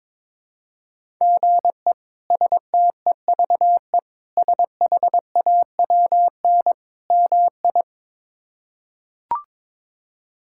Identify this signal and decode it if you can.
Morse code, heard as a keyed tone, about 22 wpm, 700 Hz. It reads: GE STEVE SHAWN MI
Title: K1USN SST QSO Practice - 22wpm